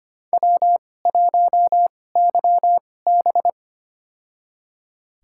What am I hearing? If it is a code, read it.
Morse code, heard as a keyed tone, about 25 wpm, spaced as for 18 wpm, 700 Hz. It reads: W1YB
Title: USA Call Signs - 2x Character Spacing - 25wpm